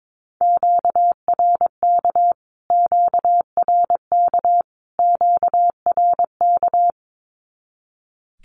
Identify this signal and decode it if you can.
Morse code, heard as a keyed tone, about 22 wpm, 700 Hz. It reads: QRK QRK QRK
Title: Q-Codes Repeated 3x 22wpm